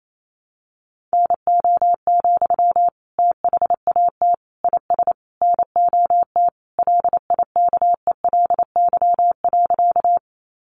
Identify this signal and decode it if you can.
Morse code, heard as a keyed tone, about 28 wpm, 700 Hz. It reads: NO, THAT IS NOT LIKELY.
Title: Sentences from Top 600 Words 28wpm